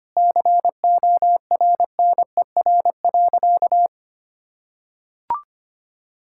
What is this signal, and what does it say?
Morse code, heard as a keyed tone, about 25 wpm, 700 Hz. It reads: CORNER.
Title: Sentences from Top 1600 Words 25wpm